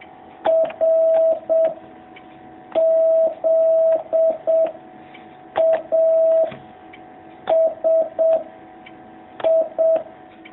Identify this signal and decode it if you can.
Morse code, heard as a keyed tone, about 7 wpm, 650 Hz. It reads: R Z A S I
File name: jh8cbh / 2019cw